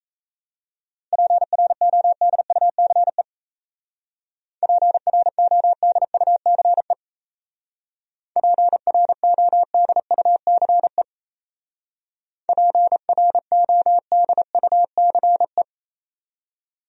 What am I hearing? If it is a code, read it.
Morse code, heard as a keyed tone, about 33 wpm, 700 Hz. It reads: PRODUCE PRODUCE PRODUCE PRODUCE
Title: Speed-Racing Top 500 Words 28wpm